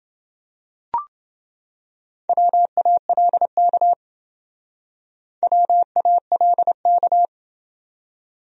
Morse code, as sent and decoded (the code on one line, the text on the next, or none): .-- .- .-.. -.- / .-- .- .-.. -.-
WALK WALK